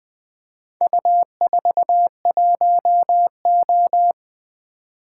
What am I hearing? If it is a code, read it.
Morse code, heard as a keyed tone, about 20 wpm, 700 Hz. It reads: U41O
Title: Four Letter-Number Combinations - 20wpm